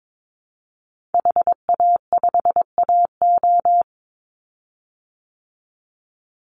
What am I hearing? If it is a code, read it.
Morse code, heard as a keyed tone, about 22 wpm, 700 Hz. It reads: HA5AO
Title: Call Signs Repeated 3x - 22wpm